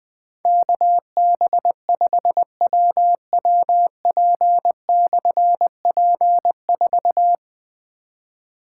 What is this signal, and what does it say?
Morse code, heard as a keyed tone, about 20 wpm, 700 Hz. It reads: KB5WWP/P4
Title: Difficult Call Signs 20wpm